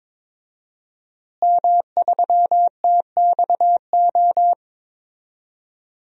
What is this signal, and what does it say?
Morse code, heard as a keyed tone, about 22 wpm, 700 Hz. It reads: M3TXO
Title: DXCC-100 Call Signs - 5 Characters - Repeated 2x - 22wpm